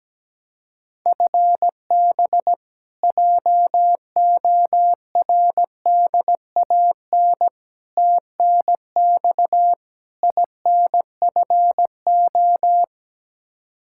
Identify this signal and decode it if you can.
Morse code, heard as a keyed tone, about 17 wpm, 700 Hz. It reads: FB JORDAN TNX INFO
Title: Practice QSO - Exchange 3 - More Info - 17wpm